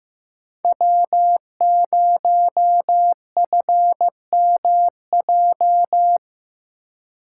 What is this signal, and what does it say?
Morse code, heard as a keyed tone, about 18 wpm, 700 Hz. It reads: W0FMJ
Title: Speed Racing - USA 1x3 Call Signs 15wpm